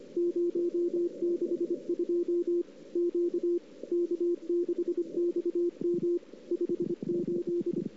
Morse code, heard as a keyed tone, about 25 wpm, 350 Hz. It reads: 062 Q K6XM 58
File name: CW / 2008-ss-cw-k5tr-op-n5rz -04:00